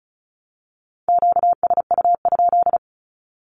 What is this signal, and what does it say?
Morse code, heard as a keyed tone, about 34 wpm, 700 Hz. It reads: QSU?
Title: Q-Codes 35wpm